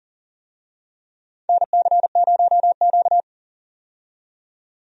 Morse code, heard as a keyed tone, about 40 wpm, 700 Hz. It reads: NC0Q
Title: USA Call Signs - 4 Characters - 40wpm